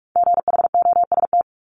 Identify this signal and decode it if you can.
Morse code, heard as a keyed tone, about 46 wpm, 690 Hz. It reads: GHOST
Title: Multi-letter Phonemes Repeated 3x 45wpm